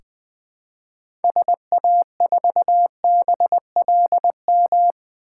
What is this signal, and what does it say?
Morse code, heard as a keyed tone, about 20 wpm, 700 Hz. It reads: SA4BLM
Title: 2023 SOTA Chaser Call Signs - 20wpm